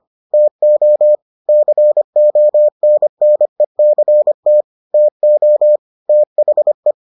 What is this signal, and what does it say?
Morse code, heard as a keyed tone, about 25 wpm, 600 Hz. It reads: TO CONNECT TO THE